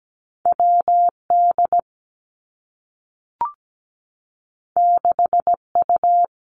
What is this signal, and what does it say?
Morse code, heard as a keyed tone, about 17 wpm, 700 Hz. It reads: WD 6U
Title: Two Challenging Letter-Number Combinations 17wpm